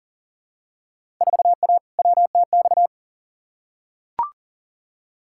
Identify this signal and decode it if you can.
Morse code, heard as a keyed tone, about 40 wpm, 700 Hz. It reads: 4A WTX
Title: Field Day Exchanges Encore - 40wpm